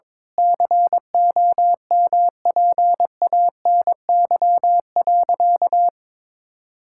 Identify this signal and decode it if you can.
Morse code, heard as a keyed tone, about 22 wpm, 700 Hz. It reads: COMPANY.
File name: practice / 22wpm / Sentences with 8-Word Phrases - 22wpm